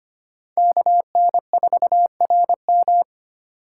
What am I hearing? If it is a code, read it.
Morse code, heard as a keyed tone, about 25 wpm, 700 Hz. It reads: KN4RM
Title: USA Call Signs - 3x Character Spacing - 25wpm